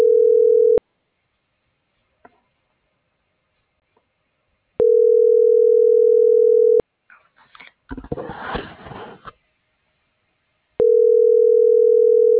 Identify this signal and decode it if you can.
Ringback tone (North America), 440 + 480 Hz.